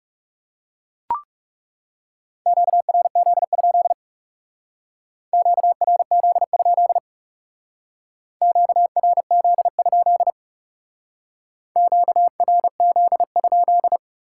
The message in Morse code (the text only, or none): QRZ? QRZ? QRZ? QRZ?